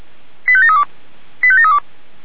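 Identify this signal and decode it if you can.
Five-tone selective call (ZVEI), heard as quick runs of five tones, tones of about 80 ms. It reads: 85732 85732
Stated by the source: ZVEI five-tone selective call received on an SDR.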